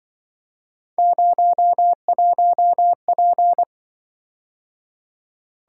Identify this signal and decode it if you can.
Morse code, heard as a keyed tone, about 25 wpm, 700 Hz. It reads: 01P